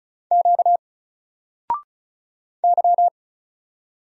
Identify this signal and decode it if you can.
Morse code, heard as a keyed tone, about 36 wpm, 700 Hz. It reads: Q Y
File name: Single-Letter-Number-35wpm